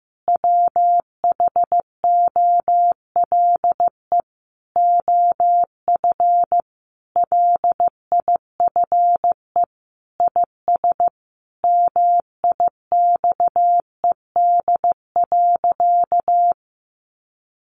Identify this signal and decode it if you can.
Morse code, heard as a keyed tone, about 15 wpm, 700 Hz. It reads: WHOLE OF LIFE IS MIXED.
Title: Sentences from Top 1500 Words 15wpm